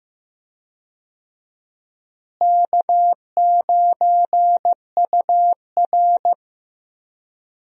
Morse code, heard as a keyed tone, about 15 wpm, 700 Hz. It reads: K9UR
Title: USA Call Signs - 4 Characters - Repeated 2x - 15wpm